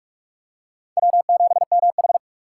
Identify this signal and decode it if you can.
Morse code, heard as a keyed tone, about 46 wpm, 700 Hz. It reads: W7MH
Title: USA Call Signs - 4 Characters - 3x Character Spacing - 45wpm